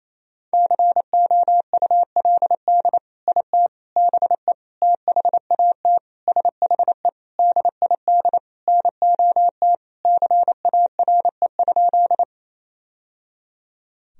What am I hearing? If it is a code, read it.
Morse code, heard as a keyed tone, about 28 wpm, 700 Hz. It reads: COULD IT BE THAT SHE DID NOT CARE?
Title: Sentences from Top 1000 Words - Review Part 1 - 28wpm